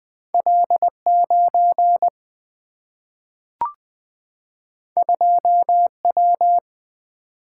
Morse code, heard as a keyed tone, about 20 wpm, 700 Hz. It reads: L9 2W
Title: Single-Letter Single-Number 20wpm